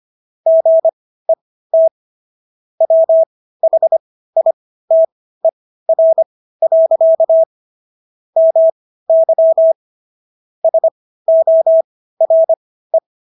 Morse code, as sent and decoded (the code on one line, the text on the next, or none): --. - / .-- .... .. - . .-. .-.-.- / -- -.-- / ... --- .-. .
GET WHITER. MY SORE